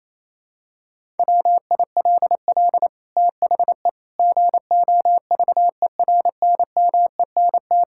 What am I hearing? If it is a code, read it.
Morse code, heard as a keyed tone, about 28 wpm, 700 Hz. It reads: WILL THE GOVERNMENT
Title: Sentences from Top 1000 Words - Review Part 5 - 28wpm